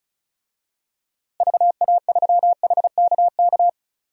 Morse code, heard as a keyed tone, about 34 wpm, 700 Hz. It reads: VA3HKK
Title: DXCC-100 Call Signs - 6 Characters - Repeated 2x - 35wpm